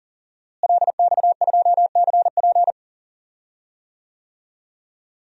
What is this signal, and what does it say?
Morse code, heard as a keyed tone, about 40 wpm, 700 Hz. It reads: LX2CP
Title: DXCC-100 Call Signs - 5 Characters - Repeated 2x - 40wpm